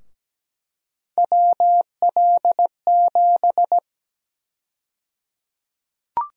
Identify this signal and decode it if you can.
Morse code, heard as a keyed tone, about 17 wpm, 700 Hz. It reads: WL7
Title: USA Call Sign Prefixes - Repeated 2x - 17wpm